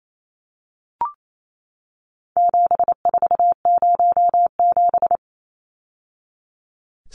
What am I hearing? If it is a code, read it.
Morse code, heard as a keyed tone, about 28 wpm, 700 Hz. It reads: 7407
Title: Numbers - 4-Digit 28wpm